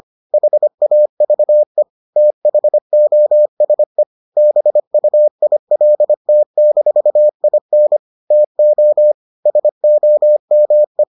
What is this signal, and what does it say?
Morse code, heard as a keyed tone, about 25 wpm, 600 Hz. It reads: HAVE THOSE BUILT-IN TO SOME